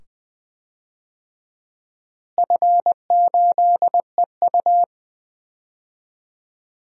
Morse code, heard as a keyed tone, about 20 wpm, 700 Hz. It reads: F8EU
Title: DXCC-100 Call Signs - 4 Characters - Repeated 2x - 20wpm